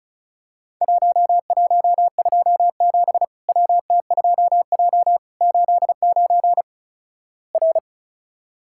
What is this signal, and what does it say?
Morse code, heard as a keyed tone, about 36 wpm, 700 Hz. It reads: 1127 WT2J 89 R
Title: Worked All Europe DX Contest - QTC Exchange - 35wpm